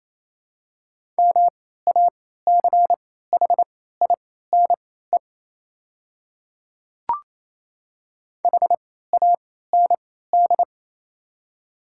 Morse code, heard as a keyed tone, about 28 wpm, spaced as for 16 wpm, 700 Hz. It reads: MACHINE HAND
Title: 3x Character Spacing - Top 500 Words - 28wpm